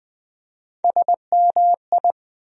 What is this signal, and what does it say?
Morse code, heard as a keyed tone, about 20 wpm, 700 Hz. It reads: SMI
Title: USA Call Sign Postfixes - 3 Letters 20wpm